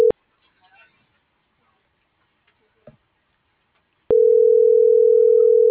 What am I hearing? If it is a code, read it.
Ringback tone (North America), 440 + 480 Hz.